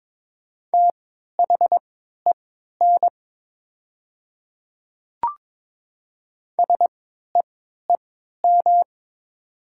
Morse code, heard as a keyed tone, about 22 wpm, spaced as for 13 wpm, 700 Hz. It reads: THEN SEEM
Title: 3x Character Spacing - Top 500 Words - 22wpm